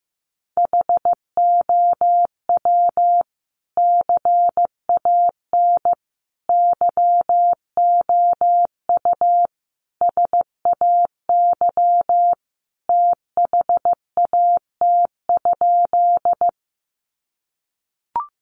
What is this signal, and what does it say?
Morse code, heard as a keyed tone, about 15 wpm, 700 Hz. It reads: HOW CAN YOU SAY THAT?
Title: Sentences from Top 200 Words 15wpm